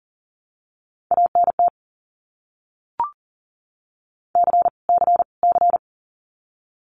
Morse code, heard as a keyed tone, about 40 wpm, 700 Hz. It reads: ANT C C C
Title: Abbreviations Repeated 3x - 40wpm